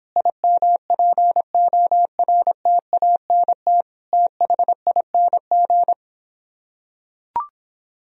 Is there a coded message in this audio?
Yes: Morse code from 0.2 to 5.9 s, heard as a keyed tone, about 26 wpm, 700 Hz. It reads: IMPORTANT THING